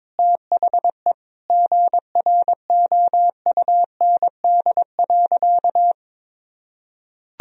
Morse code, heard as a keyed tone, about 22 wpm, 700 Hz. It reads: THE GROUND.
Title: Sentences from Top 1000 Words - Review Part 2 - 22wpm